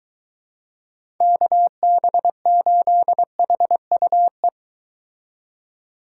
Morse code, heard as a keyed tone, about 23 wpm, 700 Hz. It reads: KB8HUE